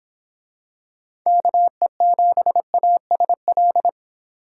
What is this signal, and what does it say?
Morse code, heard as a keyed tone, about 26 wpm, 700 Hz. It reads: KE7ASL